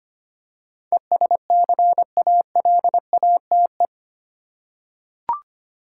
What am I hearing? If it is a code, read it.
Morse code, heard as a keyed tone, about 25 wpm, 700 Hz. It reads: ESCALATE
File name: Eight Letter Words - 25wpm